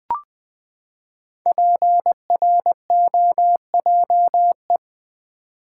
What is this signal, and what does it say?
Morse code, heard as a keyed tone, about 20 wpm, 700 Hz. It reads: PROJE
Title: Build A Word - 5 Letters At A Time - 10-Letter Words - 20wpm